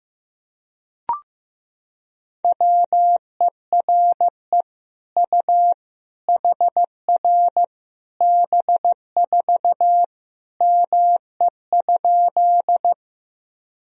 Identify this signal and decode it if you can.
Morse code, heard as a keyed tone, about 15 wpm, 700 Hz. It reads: WERE U HR B4 ME?
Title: Sentences from Top 200 Words and Abbreviations 15wpm